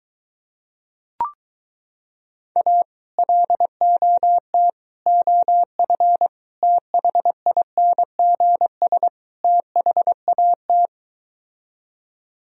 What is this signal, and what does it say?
Morse code, heard as a keyed tone, about 23 wpm, 700 Hz. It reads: A LOT OF THINGS THAT